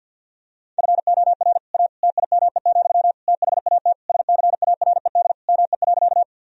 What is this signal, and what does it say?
Morse code, heard as a keyed tone, about 50 wpm, 700 Hz. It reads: FOR A TIME, THAT SCARED ME.
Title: Sentences from Top 1000 Words - Review Part 4 - 50wpm